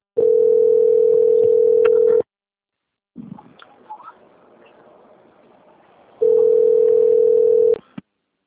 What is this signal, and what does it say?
Ringback tone (North America), 440 + 480 Hz.